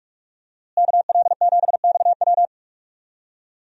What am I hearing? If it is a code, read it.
Morse code, heard as a keyed tone, about 46 wpm, 700 Hz. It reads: KL7XW
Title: DXCC-100 Call Signs - 5 Characters - Repeated 2x - 45wpm